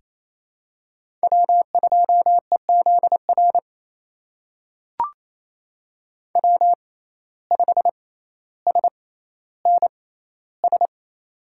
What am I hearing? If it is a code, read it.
Morse code, heard as a keyed tone, about 28 wpm, spaced as for 14 wpm, 700 Hz. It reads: W2EZR W5SNS
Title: USA Call Signs - 5 Characters - 6x Character Spacing - 28wpm